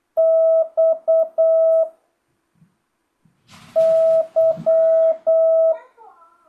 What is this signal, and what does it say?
Morse code, heard as a keyed tone, about 8 wpm, 650 Hz. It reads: X Y
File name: jh8cbh / 2025cw